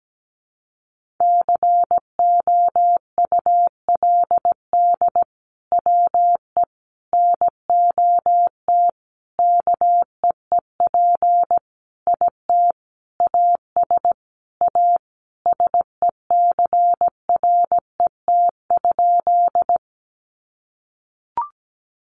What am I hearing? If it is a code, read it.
Morse code, heard as a keyed tone, about 17 wpm, 700 Hz. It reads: COULD WE NOT KEEP IT AS A SECRET?
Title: Sentences from Top 1200 Words 17wpm